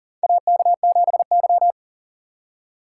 Morse code, heard as a keyed tone, about 40 wpm, 700 Hz. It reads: AK7Y
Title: USA Call Signs - 4 Characters - 6x Character Spacing - 40wpm